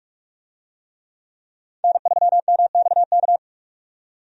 Morse code, heard as a keyed tone, about 43 wpm, 700 Hz. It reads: N3MXK
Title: USA Call Signs - 5 Characters - 45wpm